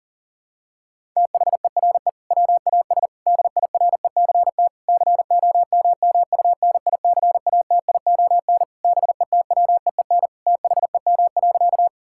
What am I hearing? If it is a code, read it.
Morse code, heard as a keyed tone, about 40 wpm, 700 Hz. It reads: THERE WAS DIRECT COMMUNICATION BETWEEN THEM.
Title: Sentences from Top 1000 Words - Review Part 6 - 40wpm